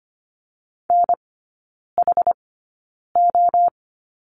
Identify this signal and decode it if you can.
Morse code, heard as a keyed tone, about 25 wpm, spaced as for 9 wpm, 700 Hz. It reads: NHO
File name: course / 25:10wpm / 093 Lesson 16e - Letter c - Three Characters 25wpm